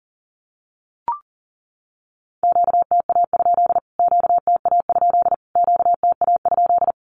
Morse code, heard as a keyed tone, about 40 wpm, 700 Hz. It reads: QTA? QTA? QTA?
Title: Q-Codes Repeated 3x 40wpm